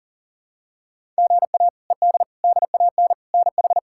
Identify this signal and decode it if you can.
Morse code, heard as a keyed tone, about 40 wpm, 700 Hz. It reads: GA ED DAN NH
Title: K1USN SST QSO Practice - 40wpm